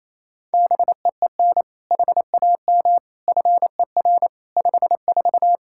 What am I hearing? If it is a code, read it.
Morse code, heard as a keyed tone, about 28 wpm, 700 Hz. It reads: BEEN HAM FER 54
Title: K1USN SST QSO Practice 28wpm